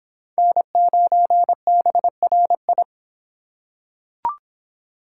Morse code, heard as a keyed tone, about 25 wpm, 700 Hz. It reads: N9BRI